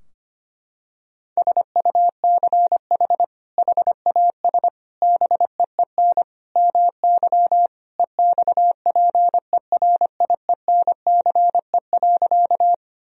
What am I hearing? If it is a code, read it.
Morse code, heard as a keyed tone, about 25 wpm, 700 Hz. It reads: SUCH HAS BEEN MY EXPERIENCE.
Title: Sentences from Top 1000 Words - Review Part 3 - 25wpm